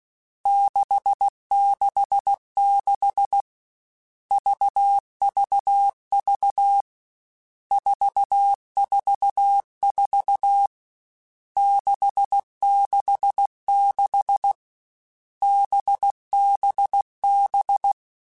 Morse code, heard as a keyed tone, about 16 wpm, 800 Hz. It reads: 666 VVV 444 666 BBB
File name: Groups@13 / Groups@13 mp3s / VB46@13